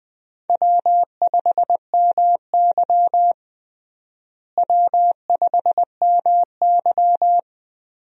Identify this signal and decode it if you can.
Morse code, heard as a keyed tone, about 20 wpm, 700 Hz. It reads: W5MY W5MY
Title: USA Call Signs - 4 Characters - Repeated 3x - 20wpm